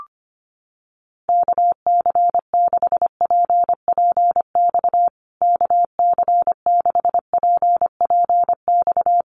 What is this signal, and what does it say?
Morse code, heard as a keyed tone, about 25 wpm, 700 Hz. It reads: KC6PPX KC6PPX
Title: USA Call Signs Repeated 3x 25wpm